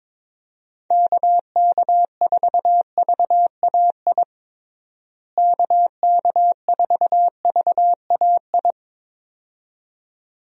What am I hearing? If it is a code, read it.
Morse code, heard as a keyed tone, about 23 wpm, 700 Hz. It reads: KK4VAI KK4VAI
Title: USA Call Signs - 6 Characters - Repeated 2x - 22wpm